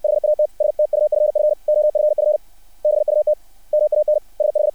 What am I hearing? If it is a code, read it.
Morse code, heard as a keyed tone, about 60 wpm, 600 Hz. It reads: HAT STUFF OFF BUT MAN SH
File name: archived / CRSnet